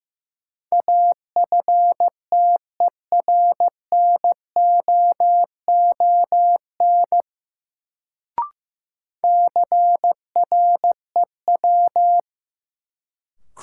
Morse code, heard as a keyed tone, about 15 wpm, 700 Hz. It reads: AFTERNOON CREW
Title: Sentences from Top 1500 Words - New Word Drill - 15wpm